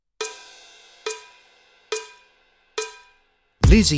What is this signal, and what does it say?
There is no code.